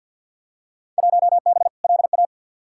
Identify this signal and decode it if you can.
Morse code, heard as a keyed tone, about 52 wpm, 700 Hz. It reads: 1B LA
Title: Field Day Exchanges - 50wpm